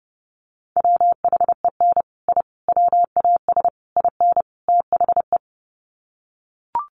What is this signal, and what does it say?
Morse code, heard as a keyed tone, about 30 wpm, 700 Hz. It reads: WHEN I WAS IN THE
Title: Sets of 5 Words Encore2 30wpm